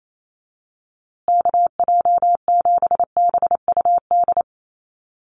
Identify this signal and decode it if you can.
Morse code, heard as a keyed tone, about 28 wpm, 700 Hz. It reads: KJ7BUD
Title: USA Call Signs Repeated 2x 28wpm